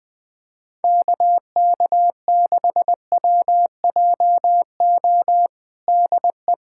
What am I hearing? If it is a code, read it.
Morse code, heard as a keyed tone, about 20 wpm, 700 Hz. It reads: KK6WJO DE
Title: USA Callsign DE Callsign 20wpm